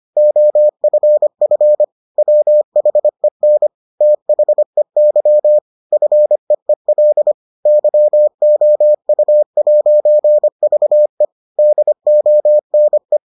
Morse code, heard as a keyed tone, about 25 wpm, 600 Hz. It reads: OFF WHEN THEY FEEL YOU'VE DONE